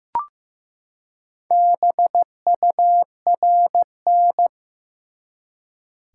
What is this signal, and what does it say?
Morse code, heard as a keyed tone, about 15 wpm, 700 Hz. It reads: BURN
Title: Four Letter Words - 15wpm